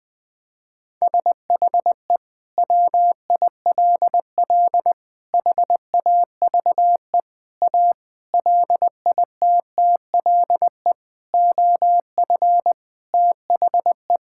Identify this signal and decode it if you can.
Morse code, heard as a keyed tone, about 21 wpm, 700 Hz. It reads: SHE WILL HAVE A LITTLE OF THE
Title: Sentences from Top 1000 Words - Review Part 5 - 20wpm